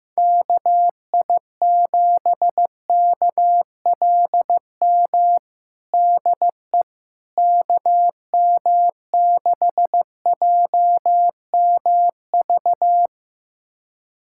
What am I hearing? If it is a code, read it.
Morse code, heard as a keyed tone, about 14 wpm, 700 Hz. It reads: KI7KLM DE KM6JMV
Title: USA Callsign DE Callsign - 6 Characters - 15wpm